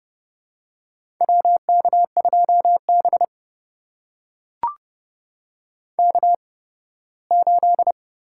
Morse code, heard as a keyed tone, about 32 wpm, spaced as for 18 wpm, 700 Hz. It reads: WK2B K8
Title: USA Call Signs - 4 Characters - 8x Character Spacing - 30wpm